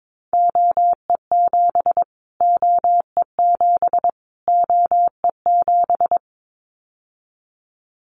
Morse code, heard as a keyed tone, about 22 wpm, 700 Hz. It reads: OE7 OE7 OE7
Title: Callsign Prefixes for Top 25 DXCC - Repeated 3x - 22wpm